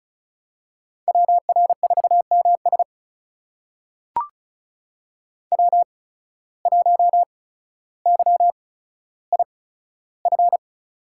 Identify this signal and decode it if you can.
Morse code, heard as a keyed tone, about 36 wpm, spaced as for 15 wpm, 700 Hz. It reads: WR4MS W1YIF
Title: USA Call Signs - 5 Characters - 8x Character Spacing - 35wpm